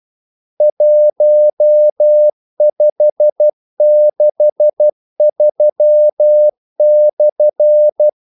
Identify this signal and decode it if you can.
Morse code, heard as a keyed tone, about 12 wpm, 600 Hz. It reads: 1563/
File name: code-exemple